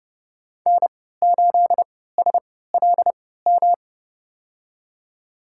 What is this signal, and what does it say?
Morse code, heard as a keyed tone, about 30 wpm, spaced as for 17 wpm, 700 Hz. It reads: N8SLM
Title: USA Call Signs - 5 Characters - 3x Character Spacing - 30wpm